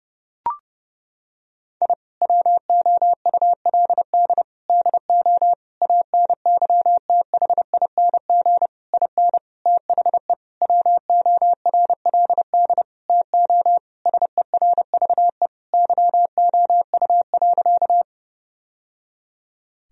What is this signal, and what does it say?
Morse code, heard as a keyed tone, about 30 wpm, 700 Hz. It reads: I WOULD DO ANYTHING IN THE WORLD TO SERVE YOU.
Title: Sentences from Top 1000 Words - Review Part 2 - 30wpm